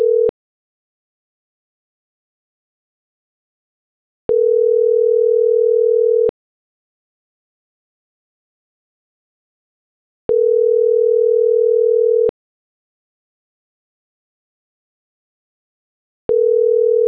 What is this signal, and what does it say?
Ringback tone (North America), 440 + 480 Hz.